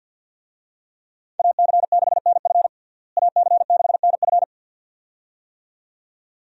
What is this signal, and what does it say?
Morse code, heard as a keyed tone, about 51 wpm, 700 Hz. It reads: AC6NF AC6NF
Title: USA Call Signs - 5 Characters - Repeated 2x - 50wpm